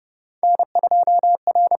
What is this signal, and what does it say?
Morse code, heard as a keyed tone, about 30 wpm, 700 Hz. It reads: N2R
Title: USA Call Signs - 5 Characters - 6x Character Spacing - 30wpm